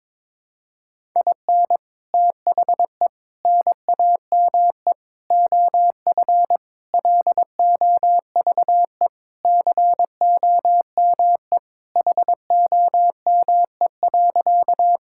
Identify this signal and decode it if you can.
Morse code, heard as a keyed tone, about 22 wpm, 700 Hz. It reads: IN THE NAME OF LOVE COME HOME.
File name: practice / 22wpm / Sentences with 3-Word Phrases - 22wpm